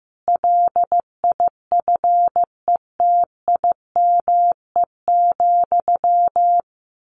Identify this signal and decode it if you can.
Morse code, heard as a keyed tone, about 15 wpm, 700 Hz. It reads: LIFETIME,
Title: A Sound Of Thunder 15wpm